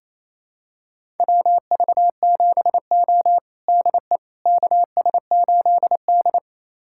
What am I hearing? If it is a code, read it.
Morse code, heard as a keyed tone, about 28 wpm, 700 Hz. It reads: WV7O DE KS8D
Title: USA Callsign DE Callsign - 4 Characters - 28wpm